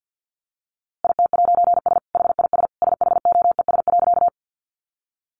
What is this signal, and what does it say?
Morse code, heard as a keyed tone, about 50 wpm, 700 Hz. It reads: IT'S HIS SHOES.
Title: A Sound Of Thunder 50wpm